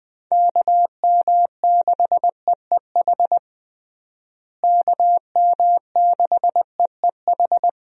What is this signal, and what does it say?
Morse code, heard as a keyed tone, about 20 wpm, 700 Hz. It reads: KM6EEH KM6EEH
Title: USA Call Signs - 6 Characters - Repeated 2x - 20wpm